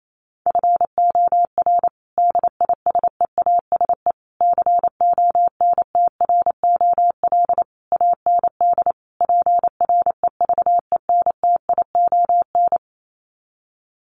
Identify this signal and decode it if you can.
Morse code, heard as a keyed tone, about 28 wpm, 700 Hz. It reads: FOR DISEASE CONTROL AND PREVENTION
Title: Sets of 5 Words Encore2 28wpm